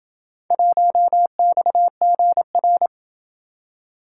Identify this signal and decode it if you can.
Morse code, heard as a keyed tone, about 27 wpm, 700 Hz. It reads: 1XGR